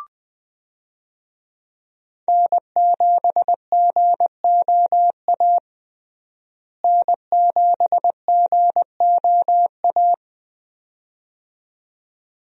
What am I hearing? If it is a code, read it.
Morse code, heard as a keyed tone, about 20 wpm, 700 Hz. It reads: N7GOA N7GOA
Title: USA Call Signs - 5 Characters - Repeated 2x - 20wpm